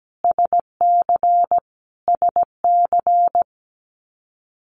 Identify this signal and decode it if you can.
Morse code, heard as a keyed tone, about 17 wpm, 700 Hz. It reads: SC SC
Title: Multi-letter Phonemes Repeated 3x 17wpm